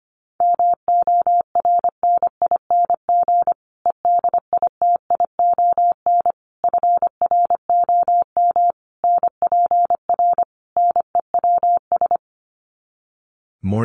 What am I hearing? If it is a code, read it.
Morse code, heard as a keyed tone, about 25 wpm, 700 Hz. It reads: MORNING EDITION FROM NPR NEWS
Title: Sets of 5 Words Encore 25wpm